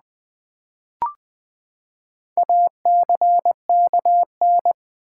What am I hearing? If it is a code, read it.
Morse code, heard as a keyed tone, about 20 wpm, 700 Hz. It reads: ACKN
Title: Build A Word - 4 Letters At A Time - 11-Letter Words - 20wpm